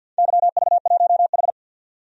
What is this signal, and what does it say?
Morse code, heard as a keyed tone, about 48 wpm, 700 Hz. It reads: YV1H
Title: DXCC-100 Call Signs - 4 Characters - 50wpm